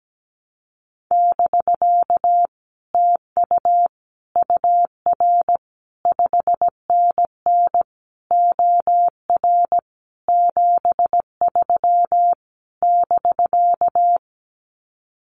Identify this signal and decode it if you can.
Morse code, heard as a keyed tone, about 17 wpm, 700 Hz. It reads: <BK> TU UR 5NN OR 73 <BK>
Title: Common QSO Phrases Encore 17wpm